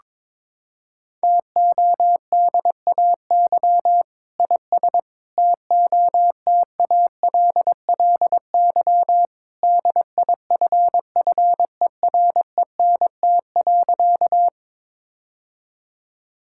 Morse code, heard as a keyed tone, about 22 wpm, 700 Hz. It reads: TODAY IS TOTALLY DIFFERENT.
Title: Sentences from Top 1600 Words 22wpm